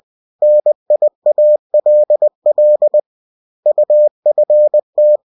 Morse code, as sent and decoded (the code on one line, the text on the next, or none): -. .. .- .-.. .-.. / ..- ..-. -
NIALL UFT